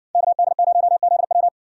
Morse code, heard as a keyed tone, about 58 wpm, 700 Hz. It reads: KD0ZW